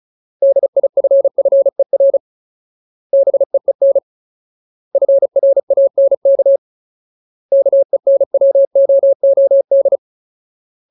Morse code, heard as a keyed tone, about 34 wpm, 550 Hz. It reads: DIFFER BEEN FRANK KENWOOD
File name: Ultimate Word List / 35WPM Ultimate